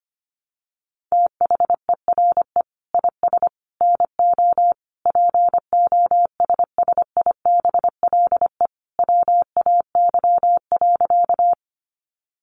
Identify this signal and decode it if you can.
Morse code, heard as a keyed tone, about 25 wpm, 700 Hz. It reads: THERE IS NO POSSIBLE WAY.
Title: Sentences from Top 500 Words 25wpm